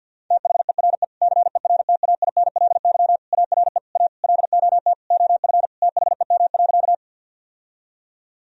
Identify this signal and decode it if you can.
Morse code, heard as a keyed tone, about 48 wpm, 700 Hz. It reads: THERE CERTAINLY ARE A LOT OF THEM.
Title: Sentences from Top 1000 Words - Review Part 3 - 50wpm